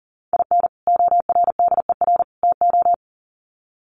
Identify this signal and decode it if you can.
Morse code, heard as a keyed tone, about 40 wpm, 700 Hz. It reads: IN ORDER TO
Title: Sets of 3 Words 40wpm